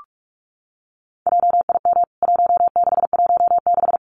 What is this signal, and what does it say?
Morse code, heard as a keyed tone, about 44 wpm, 700 Hz. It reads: JIM 1616
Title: CWT Exchange 45wpm